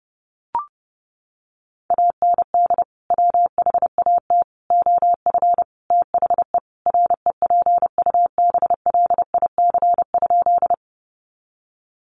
Morse code, heard as a keyed tone, about 28 wpm, 700 Hz. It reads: AND WHAT OF THE REPUBLIC?
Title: Sentences from Top 1500 Words 30wpm